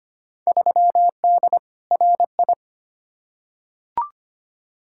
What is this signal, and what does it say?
Morse code, heard as a keyed tone, about 25 wpm, 700 Hz. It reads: 3D RI
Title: Field Day Exchanges Encore - 25wpm